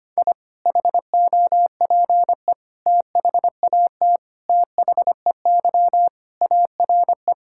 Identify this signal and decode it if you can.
Morse code, heard as a keyed tone, about 25 wpm, 700 Hz. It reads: I HOPE THAT THEY ARE
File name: practice / 25wpm / Sentences with 3-Word Phrases - 25wpm